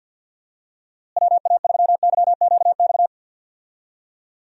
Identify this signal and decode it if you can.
Morse code, heard as a keyed tone, about 51 wpm, 700 Hz. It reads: WA3YQX
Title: USA Call Signs - 6 Characters - Repeated 2x - 50wpm